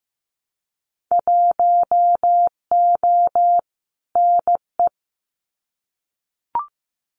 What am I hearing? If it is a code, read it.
Morse code, heard as a keyed tone, about 15 wpm, 700 Hz. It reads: 1O NE
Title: Winter Field Day Exchanges 15wpm